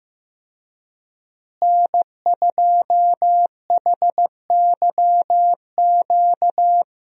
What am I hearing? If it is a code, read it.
Morse code, heard as a keyed tone, about 15 wpm, 700 Hz. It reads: N2HYQ
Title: USA Call Signs - 5 Characters - Repeated 2x - 15wpm